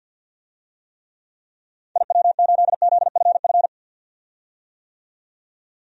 Morse code, heard as a keyed tone, about 53 wpm, 700 Hz. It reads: IW8ZFF
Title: DXCC-100 Call Signs - 6 Characters - 50wpm